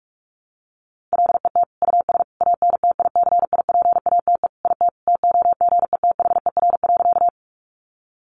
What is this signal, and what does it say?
Morse code, heard as a keyed tone, about 44 wpm, 700 Hz. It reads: LET US ANTICIPATE IT TOGETHER.
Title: Sentences from Top 1500 Words 45wpm